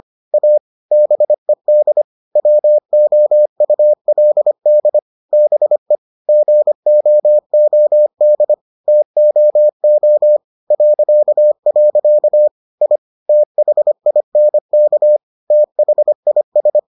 Morse code, heard as a keyed tone, about 25 wpm, 600 Hz. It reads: A BED WOULD BE GOOD TOO .. I THINK THIS